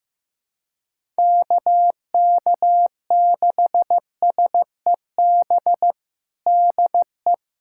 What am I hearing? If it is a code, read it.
Morse code, heard as a keyed tone, about 15 wpm, 700 Hz. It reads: KK6SEB DE
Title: USA Callsign DE Callsign - 6 Characters - 15wpm